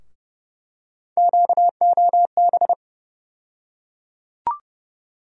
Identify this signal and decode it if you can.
Morse code, heard as a keyed tone, about 32 wpm, 700 Hz. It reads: QOB
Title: USA Call Sign Postfixes - 3 Letters 30wpm